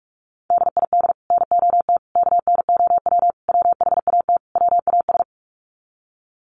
Morse code, heard as a keyed tone, about 44 wpm, 700 Hz. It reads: DID NOT KNOW WHAT WAS
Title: Sets of 5 Words Encore2 45wpm